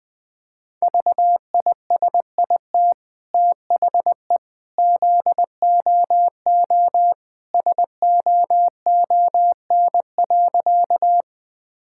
Morse code, heard as a keyed tone, about 20 wpm, 700 Hz. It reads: VISIT THE ZOO SOON.
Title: Easy Sentences with Hints - 20wpm